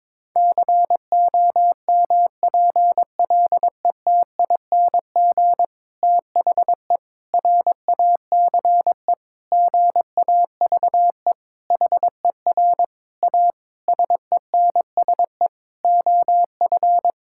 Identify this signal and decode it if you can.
Morse code, heard as a keyed tone, about 22 wpm, 700 Hz. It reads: COMPLETING THE RACE GAVE HER A SENSE OF
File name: Difficult Sentences with Hints - 22wpm